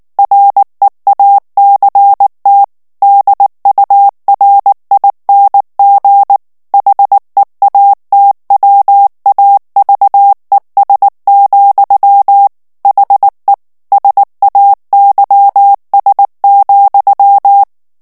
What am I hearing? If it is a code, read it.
Morse code, heard as a keyed tone, about 19 wpm, 800 Hz. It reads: REACT DURING HEATWAVES, HE SAYS,
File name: morse mp3 / ANGLAIS texte clair 2 vitesse 18 mots minute